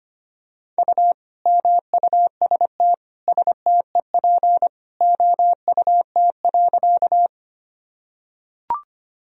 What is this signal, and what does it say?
Morse code, heard as a keyed tone, about 25 wpm, 700 Hz. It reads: U MUST STEP OUT.
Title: Sentences from Top 400 Words and Abbreviations 25wpm